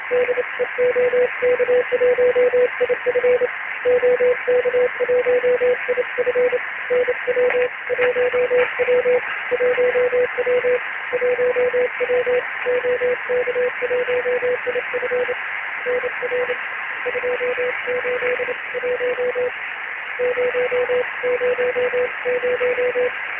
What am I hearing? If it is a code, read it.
Morse code, heard as a keyed tone, about 28 wpm, 500 Hz, with noise. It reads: DE OK1IF OK1IF NW 1W 1W 1W OK1IF NR 281 000
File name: ARCHIV / BPSK CW / OK1IF-OK1ANG- -1230CET-3766.5-3767